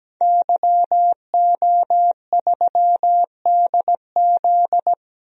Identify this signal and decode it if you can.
Morse code, heard as a keyed tone, about 17 wpm, 700 Hz. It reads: YO3DZ
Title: DXCC-100 Call Signs - 5 Characters - 17wpm